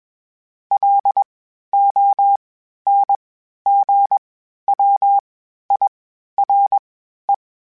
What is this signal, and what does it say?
Morse code, heard as a keyed tone, about 21 wpm, spaced as for 12 wpm, 800 Hz. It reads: LONGWIRE
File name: (spaced) / FromSMs spaced mp3s / Antennas sp@25